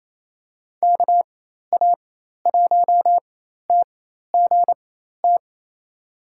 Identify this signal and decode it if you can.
Morse code, heard as a keyed tone, about 28 wpm, spaced as for 13 wpm, 700 Hz. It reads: KA1TGT
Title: USA Call Signs - 6 Characters - 4x Character Spacing - 28wpm